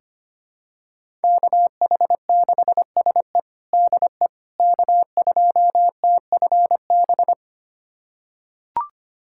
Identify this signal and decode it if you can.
Morse code, heard as a keyed tone, about 25 wpm, 700 Hz. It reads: KH6SE DE K2TFB
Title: USA Callsign DE Callsign - 5 Characters - 25wpm